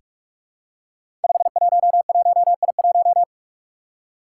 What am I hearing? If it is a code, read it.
Morse code, heard as a keyed tone, about 44 wpm, 700 Hz. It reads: 511I1
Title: Five Challenging Letter-Number Combinations 45wpm